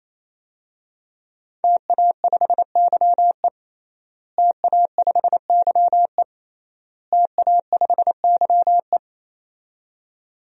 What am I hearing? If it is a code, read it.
Morse code, heard as a keyed tone, about 27 wpm, 700 Hz. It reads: TA5YE TA5YE TA5YE
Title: DXCC-100 Call Signs - 5 Characters - Repeated 3x - 28wpm